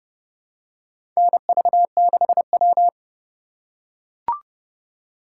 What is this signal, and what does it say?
Morse code, heard as a keyed tone, about 31 wpm, 700 Hz. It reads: NV6W
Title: USA Call Signs - 4 Characters - 4x Character Spacing - 30wpm